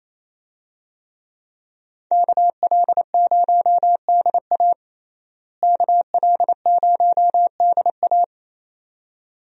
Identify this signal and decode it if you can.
Morse code, heard as a keyed tone, about 28 wpm, 700 Hz. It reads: KL0DA KL0DA
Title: USA Call Signs - 5 Characters - Repeated 2x - 28wpm